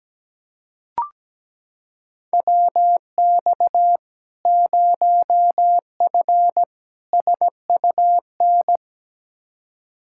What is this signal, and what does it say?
Morse code, heard as a keyed tone, about 17 wpm, 700 Hz. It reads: WX 0F SUN
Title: Practice QSO - Exchange 2 - Info - 17wpm